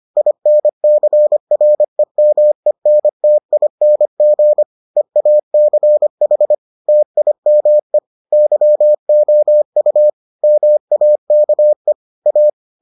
Morse code, heard as a keyed tone, about 25 wpm, 600 Hz. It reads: INCREMENTING EACH TIME YOU MAKE A